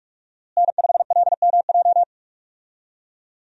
Morse code, heard as a keyed tone, about 46 wpm, 700 Hz. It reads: N5LMJ
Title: USA Call Signs - 5 Characters - Repeated 3x - 45wpm